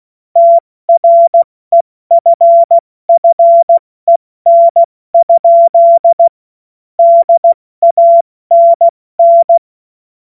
Morse code, heard as a keyed tone, about 16 wpm, 680 Hz. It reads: TREFFEN? DANN
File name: Morsekurs / Witze / Witz 24.Oktober 2023 70 F80